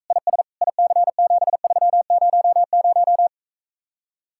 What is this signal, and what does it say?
Morse code, heard as a keyed tone, about 45 wpm, 700 Hz. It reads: IS IC7300